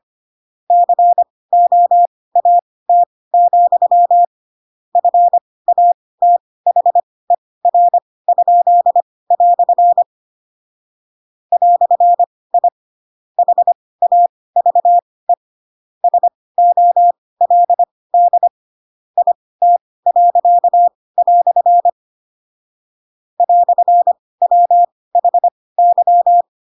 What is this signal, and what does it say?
Morse code, heard as a keyed tone, about 25 wpm, spaced as for 18 wpm, 700 Hz. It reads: COAT, FATHER?" "I HAVE SOLD IT." "WHY